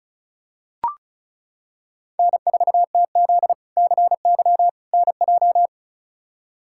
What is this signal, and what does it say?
Morse code, heard as a keyed tone, about 34 wpm, 700 Hz. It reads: N4TZ CY NJ
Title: K1USN SST QSO Practice - 35wpm